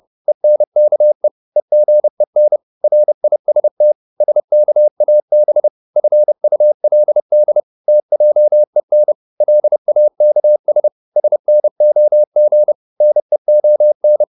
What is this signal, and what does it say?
Morse code, heard as a keyed tone, about 30 wpm, 600 Hz. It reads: ENKE EGEN RIST SKAB FULD TJEN LAKS SNOG NEON
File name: Korte ord DK 30wpm